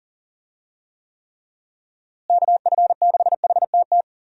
Morse code, heard as a keyed tone, about 40 wpm, 700 Hz. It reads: KF6HTT
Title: USA Call Signs - 6 Characters - Repeated 3x - 40wpm